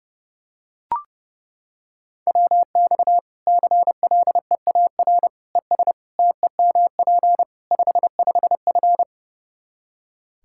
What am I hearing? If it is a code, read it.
Morse code, heard as a keyed tone, about 32 wpm, 700 Hz. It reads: WX CLEAR ES TEMP 55F
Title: Practice QSO - Exchange 2 - Info - 30wpm